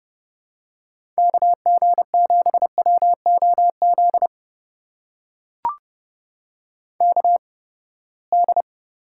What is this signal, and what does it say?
Morse code, heard as a keyed tone, about 30 wpm, spaced as for 21 wpm, 700 Hz. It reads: KG7WOZ KD
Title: USA Call Signs - 6 Characters - 8x Character Spacing - 30wpm